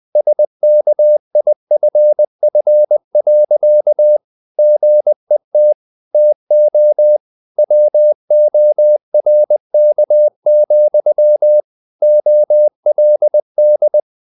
Morse code, heard as a keyed tone, about 20 wpm, 600 Hz. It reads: SKIFF. GET TO WORK, OLD
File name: the old man and the sea part 2 20wpm